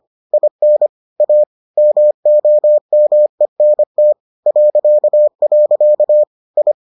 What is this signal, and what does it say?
Morse code, heard as a keyed tone, about 25 wpm, 600 Hz. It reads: IN A MOMENT .. I